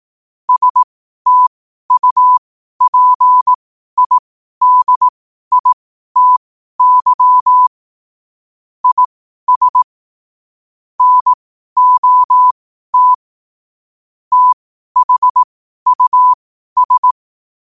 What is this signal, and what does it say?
Morse code, heard as a keyed tone, about 18 wpm, spaced as for 12 wpm, 1000 Hz. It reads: STUPIDITY IS NOT THUS
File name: qotdfiles / 12wpm